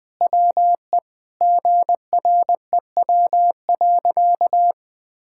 Morse code, heard as a keyed tone, about 21 wpm, 700 Hz. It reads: WE GREW.
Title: Sentences with 8-Word Phrases - Spoken Beginning - 20wpm